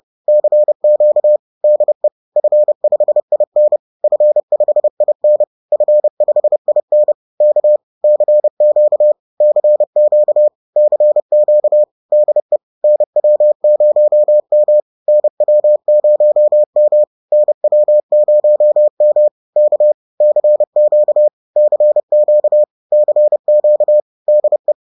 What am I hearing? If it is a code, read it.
Morse code, heard as a keyed tone, about 30 wpm, 600 Hz. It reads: CQ DE F5IN F5IN F5IN K CQ CQ CQ DE NW0M NW0M NW0M K CQ CQ CQ DE